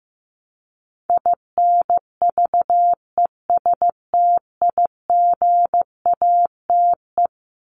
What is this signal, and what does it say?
Morse code, heard as a keyed tone, about 15 wpm, 700 Hz. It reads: INVESTIGATE
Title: Top 1751-2000 Words 15wpm